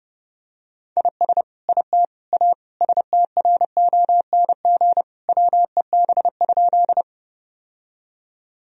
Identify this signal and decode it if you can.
Morse code, heard as a keyed tone, about 30 wpm, 700 Hz. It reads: IS IT A STRONG WEB?
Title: Sentences from Top 1000 Words - Review Part 4 - 30wpm